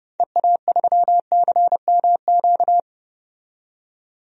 Morse code, heard as a keyed tone, about 30 wpm, 700 Hz. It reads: EA3CMQ
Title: DXCC-100 Call Signs - 6 Characters - 30wpm